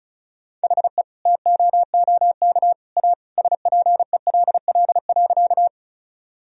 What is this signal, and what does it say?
Morse code, heard as a keyed tone, about 36 wpm, 700 Hz. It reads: HE TOOK A SPELL.
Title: Sentences from Top 1000 Words - Review Part 1 - 35wpm